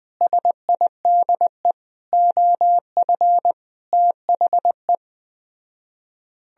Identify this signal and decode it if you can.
Morse code, heard as a keyed tone, about 20 wpm, 700 Hz. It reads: SIDE OF THE
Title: Speed-Racing - Sets of 5 Words 17wpm